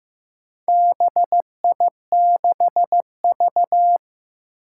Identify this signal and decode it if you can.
Morse code, heard as a keyed tone, about 15 wpm, 700 Hz. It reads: BI6V
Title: Four Challenging Letter-Number Combinations 15wpm